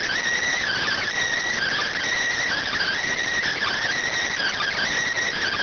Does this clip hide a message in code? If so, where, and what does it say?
Yes: Emergency Alert System (SAME) tones (header bursts) from 0.5 to 4.9 s, heard as harsh buzzing bursts.